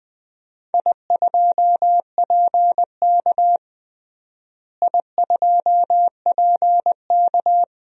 Morse code, heard as a keyed tone, about 20 wpm, 700 Hz. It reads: I2PK I2PK
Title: DXCC-100 Call Signs - 4 Characters - Repeated 3x - 20wpm